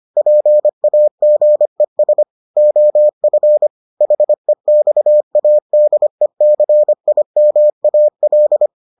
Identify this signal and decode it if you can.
Morse code, heard as a keyed tone, about 25 wpm, 600 Hz. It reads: PAGES OF HEXADECIMAL